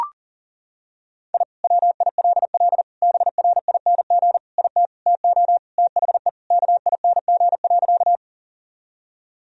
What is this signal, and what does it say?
Morse code, heard as a keyed tone, about 37 wpm, 700 Hz. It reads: I WILL BRING IT TO THE KING.
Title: Sentences from Top 1000 Words - Review Part 3 - 40wpm